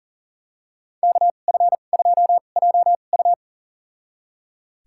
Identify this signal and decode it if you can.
Morse code, heard as a keyed tone, about 37 wpm, spaced as for 28 wpm, 700 Hz. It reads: KF2JU
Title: USA Call Signs - 5 Characters - 2x Character Spacing - 40wpm